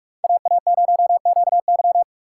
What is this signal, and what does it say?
Morse code, heard as a keyed tone, about 44 wpm, 700 Hz. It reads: AA0QY